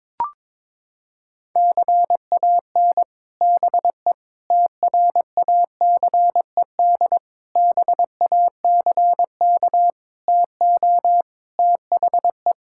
Morse code, heard as a keyed tone, about 22 wpm, 700 Hz. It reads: CAN BE TRACED BACK TO THE
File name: practice / 22wpm / Sets of 6 Words - 22wpm